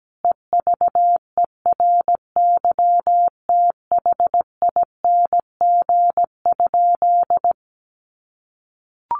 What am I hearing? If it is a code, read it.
Morse code, heard as a keyed tone, about 17 wpm, 700 Hz. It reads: EVERYTHING?
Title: Sentences from Top 700 Words 17wpm